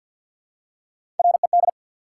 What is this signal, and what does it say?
Morse code, heard as a keyed tone, about 53 wpm, 700 Hz. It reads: RED
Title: Three Letter Words - 50wpm